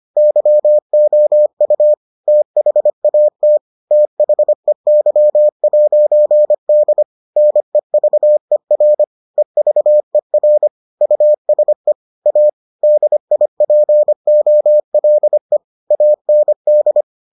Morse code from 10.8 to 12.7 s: USE A

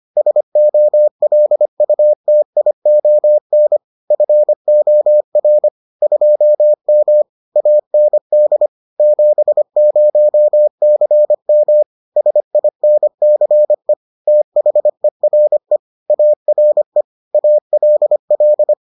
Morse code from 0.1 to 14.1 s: SOLUTION FOR 2M AND 70CM SINCE

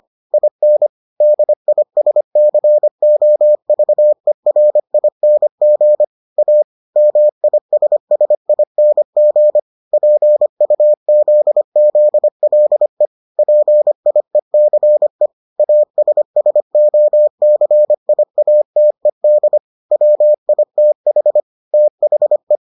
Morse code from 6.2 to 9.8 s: A MISSING

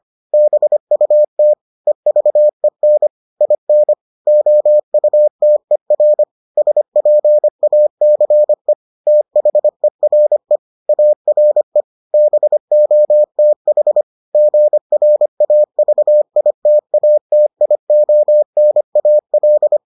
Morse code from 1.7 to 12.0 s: EVEN IN OUTER SPACE THERE ARE

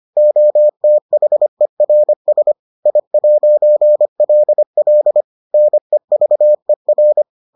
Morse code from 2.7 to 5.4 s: I'LL

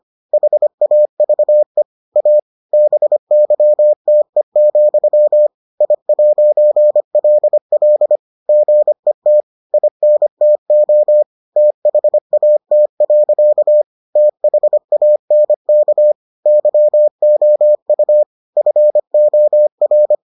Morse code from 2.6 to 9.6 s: BYTE, I'LL GET

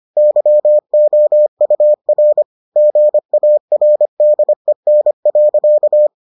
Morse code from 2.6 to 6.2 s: GARDEN.

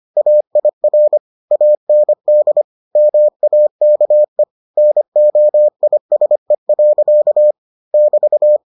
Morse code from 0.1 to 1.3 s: AIR

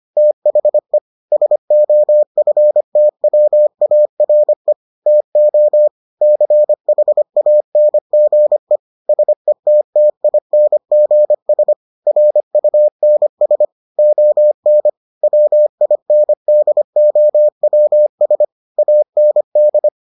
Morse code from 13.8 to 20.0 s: ON WINDOWS AND

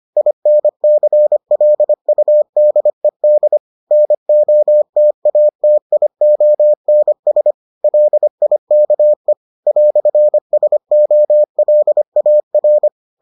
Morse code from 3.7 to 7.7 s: NOTATIONS

